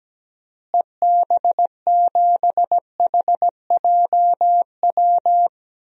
Morse code, heard as a keyed tone, about 17 wpm, 700 Hz. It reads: EB7HJW
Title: DXCC-100 Call Signs - 6 Characters - Repeated 2x - 17wpm